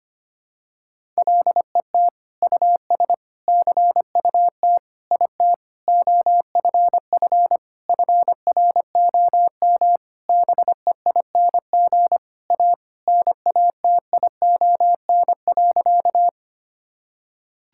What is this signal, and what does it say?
Morse code, heard as a keyed tone, about 25 wpm, 700 Hz. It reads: LET US CUT IT OFF FROM BEING A NATION.
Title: Sentences from Top 1000 Words - Review Part 6 - 25wpm